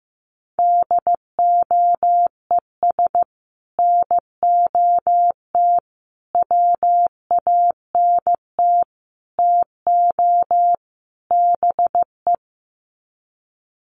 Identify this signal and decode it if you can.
Morse code, heard as a keyed tone, about 15 wpm, 700 Hz. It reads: DOES NOT WANT TO BE
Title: Sets of 5 Words Encore 15wpm